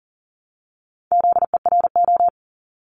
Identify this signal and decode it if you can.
Morse code, heard as a keyed tone, about 38 wpm, 690 Hz. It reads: ZERO
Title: Multi-letter Phonemes Repeated 3x 40wpm